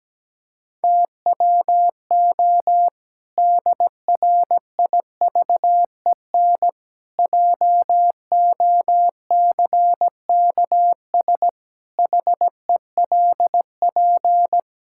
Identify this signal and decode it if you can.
Morse code, heard as a keyed tone, about 17 wpm, 700 Hz. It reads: TWO DRIVEN JOCKS HELP
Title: Pangram Challenge v2 - 17wpm